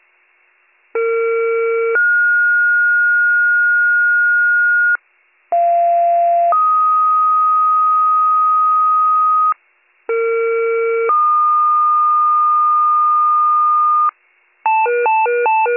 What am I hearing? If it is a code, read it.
Two-tone page, heard as a one-second tone, then a three-second tone, about 460 then 1430 Hz, about 690 then 1250 Hz, about 460 then 1220 Hz. It is heard from 0.9 to 14.1 s.